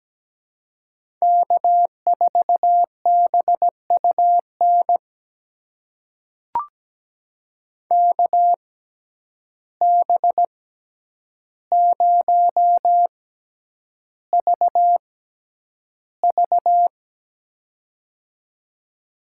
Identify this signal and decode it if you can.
Morse code, heard as a keyed tone, about 17 wpm, spaced as for 9 wpm, 700 Hz. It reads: K4BUN KB0VV
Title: USA Call Signs - 5 Characters - 6x Character Spacing - 17wpm